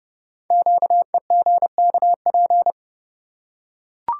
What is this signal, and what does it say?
Morse code, heard as a keyed tone, about 30 wpm, 690 Hz. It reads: QEGKP
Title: Code Groups - 5 Letters - 30wpm